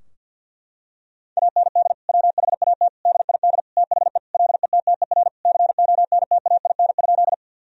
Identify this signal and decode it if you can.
Morse code, heard as a keyed tone, about 52 wpm, 700 Hz. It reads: AND WHAT DID THE LETTER CONTAIN?
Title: Sentences from Top 1000 Words - Review Part 2 - 50wpm